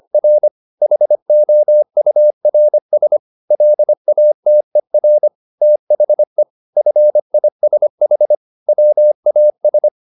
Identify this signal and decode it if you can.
Morse code, heard as a keyed tone, about 25 wpm, 600 Hz. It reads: R HOURS LATER THE FISH WAS
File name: the old man and the sea part 1 25wpm